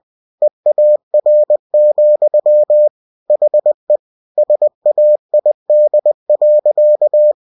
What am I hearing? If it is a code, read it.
Morse code, heard as a keyed tone, about 20 wpm, 600 Hz. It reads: EAR, HE SAID.